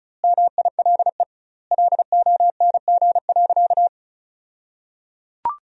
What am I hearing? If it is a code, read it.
Morse code, heard as a keyed tone, about 34 wpm, 700 Hz. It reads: MILE LONG.
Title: 2x Word-Spacing - Sentences from Top 500 Words - 35wpm